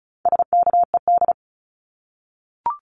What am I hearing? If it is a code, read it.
Morse code, heard as a keyed tone, about 34 wpm, 700 Hz. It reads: SKED
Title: Abbreviations Repeated 3x - 35wpm